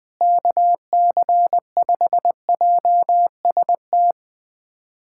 Morse code, heard as a keyed tone, about 21 wpm, 700 Hz. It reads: KC5JST
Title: USA Callsign DE Callsign - 6 Characters - 20wpm